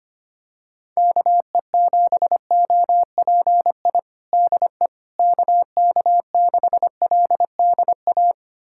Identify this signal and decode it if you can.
Morse code, heard as a keyed tone, about 25 wpm, 700 Hz. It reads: KE7OPI DE KK6LDA
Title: USA Callsign DE Callsign - 6 Characters - 25wpm